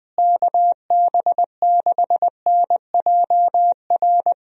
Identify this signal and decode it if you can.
Morse code, heard as a keyed tone, about 19 wpm, 700 Hz. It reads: KB6NJR
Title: USA Call Signs - 6 Characters - Repeated 3x - 20wpm